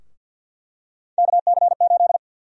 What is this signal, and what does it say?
Morse code, heard as a keyed tone, about 51 wpm, 700 Hz. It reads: KC8
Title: USA Call Sign Prefixes - 50wpm